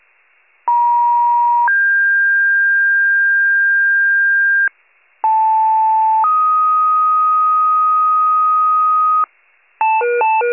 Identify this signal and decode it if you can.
Two-tone page, heard as a one-second tone, then a three-second tone, about 950 then 1600 Hz, about 880 then 1250 Hz, with noise. It is heard from 0.7 to 9.2 s.
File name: wcalert